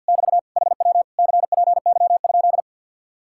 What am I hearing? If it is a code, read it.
Morse code, heard as a keyed tone, about 58 wpm, 700 Hz. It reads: <BT> HW CPY?
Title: Speed-Racing Common QSO Phrases - 50wpm